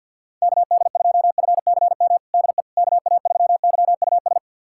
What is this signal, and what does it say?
Morse code, heard as a keyed tone, about 51 wpm, 700 Hz. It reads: KD2FCM DE KA3YUS
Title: USA Callsign DE Callsign - 6 Characters - 50wpm